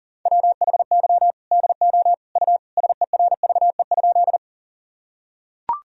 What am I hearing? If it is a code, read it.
Morse code, heard as a keyed tone, about 40 wpm, 700 Hz. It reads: WHY DO U SERVE?
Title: Sentences from Top 400 Words and Abbreviations 40wpm